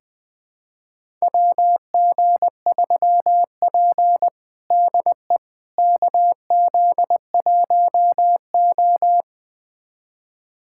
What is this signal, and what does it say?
Morse code, heard as a keyed tone, about 20 wpm, 700 Hz. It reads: WG3P DE KZ1O
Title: USA Callsign DE Callsign - 4 Characters - 20wpm